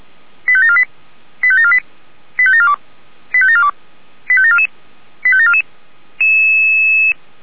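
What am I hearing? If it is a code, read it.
Five-tone selective call (ZVEI), heard as quick runs of five tones, tones of about 70 ms. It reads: 85748 85748 85742 85742 85744 85744
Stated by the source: ZVEI five-tone call received with an SDR.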